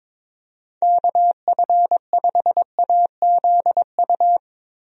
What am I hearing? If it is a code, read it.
Morse code, heard as a keyed tone, about 22 wpm, 700 Hz. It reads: KF5AZU
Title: USA Call Signs - 6 Characters - Repeated 2x - 22wpm